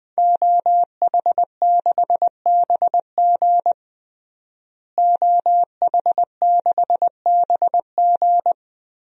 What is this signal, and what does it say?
Morse code, heard as a keyed tone, about 20 wpm, 700 Hz. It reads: OH6BG OH6BG
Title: DXCC-100 Call Signs - 5 Characters - Repeated 2x - 20wpm